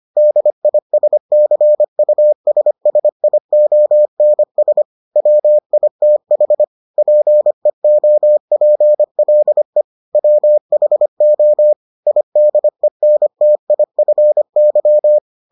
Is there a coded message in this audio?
Yes: Morse code from 0.2 to 15.2 s, heard as a keyed tone, about 25 wpm, 600 Hz. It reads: DISCUSSIONS WITH PEOPLE WHO IDENTIFY